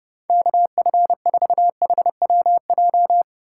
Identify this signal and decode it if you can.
Morse code, heard as a keyed tone, about 29 wpm, 700 Hz. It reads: KF4HWJ